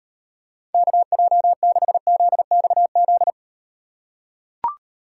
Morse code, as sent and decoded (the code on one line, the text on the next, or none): -.- .--- -.... --.. -..- --..
KJ6ZXZ